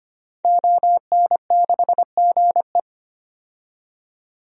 Morse code, heard as a keyed tone, about 25 wpm, 700 Hz. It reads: ON6GE
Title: DXCC-100 Call Signs - 5 Characters - 25wpm